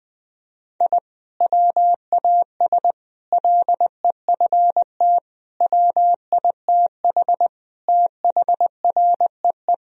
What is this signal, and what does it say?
Morse code, heard as a keyed tone, about 21 wpm, 700 Hz. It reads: I WAS LEFT WITH THREE
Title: Sentences from Top 1000 Words - Review Part 1 - 20wpm